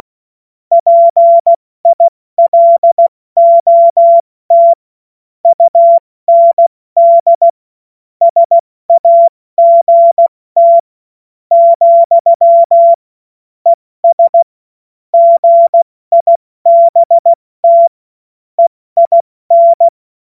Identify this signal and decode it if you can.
Morse code, heard as a keyed tone, about 16 wpm, spaced as for 13 wpm, 680 Hz. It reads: PILOT UND SAGT, ES GIBT EIN
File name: Morsekurs / Witze / Witz 16. Februar 2021, 70 F80